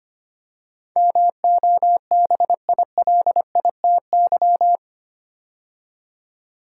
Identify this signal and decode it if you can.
Morse code, heard as a keyed tone, about 25 wpm, 700 Hz. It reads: MOBILITY
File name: 8 Letter Words - First 3 Letters Given - 25wpm